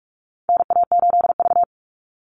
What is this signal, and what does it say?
Morse code, heard as a keyed tone, about 44 wpm, 700 Hz. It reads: NA8V
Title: Call Signs - 45wpm